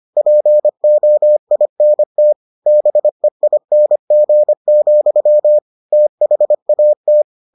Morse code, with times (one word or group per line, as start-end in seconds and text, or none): 0.2-2.3 POINT
2.7-5.6 BEING,
5.9-7.2 THAT